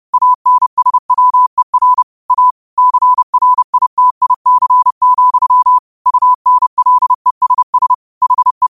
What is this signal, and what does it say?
Morse code, heard as a keyed tone, about 30 wpm, 1000 Hz. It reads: ANSWER A CRITIC, UNLESS HE